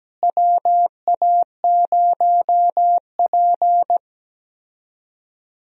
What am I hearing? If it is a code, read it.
Morse code, heard as a keyed tone, about 17 wpm, 700 Hz. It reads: WA0P
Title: USA Call Signs - 4 Characters - Repeated 2x - 17wpm